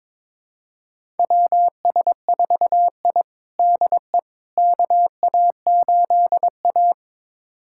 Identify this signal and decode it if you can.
Morse code, heard as a keyed tone, about 22 wpm, 700 Hz. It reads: WS4I DE KA8A
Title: USA Callsign DE Callsign - 4 Characters - 22wpm